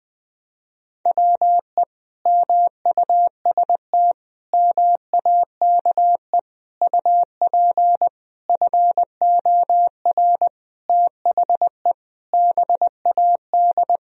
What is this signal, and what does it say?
Morse code, heard as a keyed tone, about 21 wpm, 700 Hz. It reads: WE MUST MAKE UP FOR THE BAD
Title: Sentences from Top 1000 Words - Review Part 3 - 20wpm